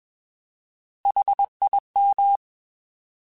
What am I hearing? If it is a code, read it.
Morse code, heard as a keyed tone, about 21 wpm, 800 Hz. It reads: HIM
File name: Alphabetical / Alphabetical mp3s / Hs @20